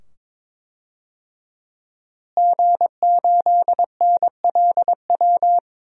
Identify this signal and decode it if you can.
Morse code, heard as a keyed tone, about 22 wpm, 700 Hz. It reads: G8NLW
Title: DXCC-100 Call Signs - 5 Characters - Repeated 2x - 22wpm